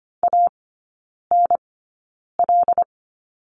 Morse code, heard as a keyed tone, about 25 wpm, spaced as for 9 wpm, 690 Hz. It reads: ANL
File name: Lesson 15e - Number 5 - Three Characters 25wpm